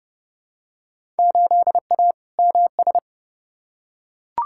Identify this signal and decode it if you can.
Morse code, heard as a keyed tone, about 30 wpm, 700 Hz. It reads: 8A MS
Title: Field Day Exchanges - 30wpm